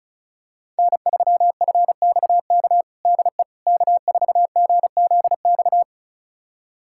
Morse code, heard as a keyed tone, about 36 wpm, 690 Hz. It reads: N3FXK DE K4GZX
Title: USA Callsign DE Callsign - 5 Characters - 35wpm